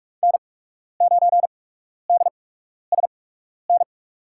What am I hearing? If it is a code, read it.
Morse code, heard as a keyed tone, about 43 wpm, spaced as for 12 wpm, 700 Hz. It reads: N9DSN
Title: USA Call Signs - 5 Characters - 8x Character Spacing - 45wpm